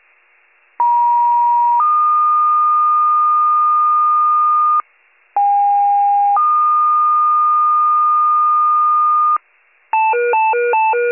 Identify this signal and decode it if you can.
Two-tone page, heard as a one-second tone, then a three-second tone, about 950 then 1220 Hz, about 810 then 1220 Hz. It is heard from 0.8 to 9.4 s.